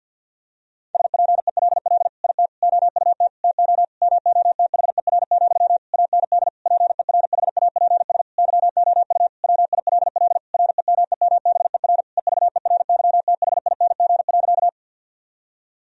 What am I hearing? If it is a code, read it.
Morse code, heard as a keyed tone, about 51 wpm, 700 Hz. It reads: SPELL IT OUT TO MOTHER, AND PERHAPS YOU WILL REMEMBER EVERYTHING.
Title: Sentences from Top 1000 Words - Review Part 1 - 50wpm